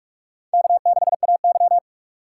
Morse code, heard as a keyed tone, about 46 wpm, 700 Hz. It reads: K6AY
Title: USA Call Signs - 4 Characters - 6x Character Spacing - 45wpm